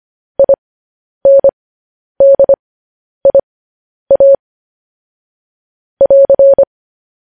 Morse code, heard as a keyed tone, about 25 wpm, spaced as for 10 wpm, 550 Hz. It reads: INDIA <AR>
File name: podcast / News headlines-1x-25x10